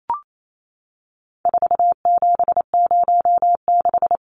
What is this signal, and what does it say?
Morse code, heard as a keyed tone, about 28 wpm, 700 Hz. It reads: 4706
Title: Numbers - 4-Digit 28wpm